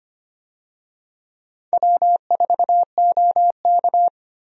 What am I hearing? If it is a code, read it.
Morse code, heard as a keyed tone, about 25 wpm, 700 Hz. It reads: W4OK
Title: USA Call Signs - 4 Characters - Repeated 3x - 25wpm